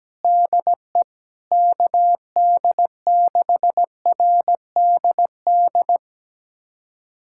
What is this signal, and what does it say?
Morse code, heard as a keyed tone, about 17 wpm, 700 Hz. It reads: DE KD6RDD
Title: USA Callsign DE Callsign - 6 Characters - 17wpm